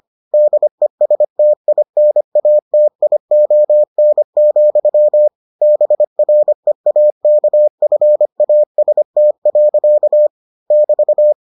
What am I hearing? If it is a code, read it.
Morse code, heard as a keyed tone, about 25 wpm, 600 Hz. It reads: DESTINATION, BREAKFAST. <BT>